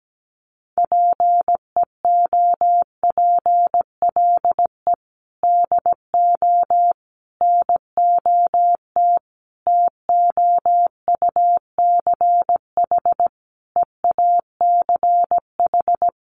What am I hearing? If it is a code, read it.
Morse code, heard as a keyed tone, about 17 wpm, 700 Hz. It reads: PEOPLE DO NOT TOUCH EACH
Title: Sentences from Top 1200 Words 17wpm